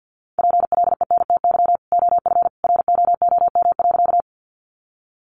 Morse code, heard as a keyed tone, about 50 wpm, 700 Hz. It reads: PLENTY OF ROOM.
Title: Sentences from Top 1300 Words 50wpm